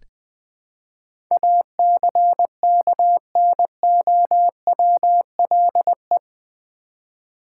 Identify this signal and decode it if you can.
Morse code, heard as a keyed tone, about 20 wpm, 700 Hz. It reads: ACKNOWLE
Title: Build A Word - 4 Letters At A Time - 11-Letter Words - 20wpm